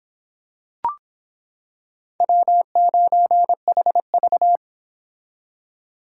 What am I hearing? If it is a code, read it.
Morse code, heard as a keyed tone, about 26 wpm, 700 Hz. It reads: W9HV